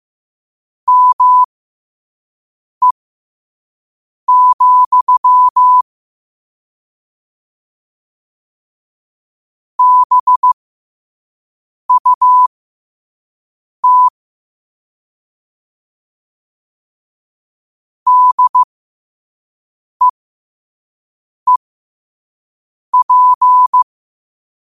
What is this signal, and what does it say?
Morse code, heard as a keyed tone, about 15 wpm, spaced as for 5 wpm, 1000 Hz. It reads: ME, BUT DEEP
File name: qotdfiles / 5wpm